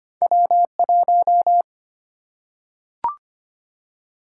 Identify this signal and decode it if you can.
Morse code, heard as a keyed tone, about 25 wpm, 700 Hz. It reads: W1
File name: Two Letter-Number Combinations - 25wpm